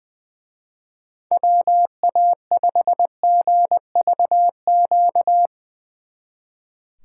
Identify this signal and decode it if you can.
Morse code, heard as a keyed tone, about 22 wpm, 700 Hz. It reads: WA5GVQ
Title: Speed Racing - USA 2x3 Call Signs 20wpm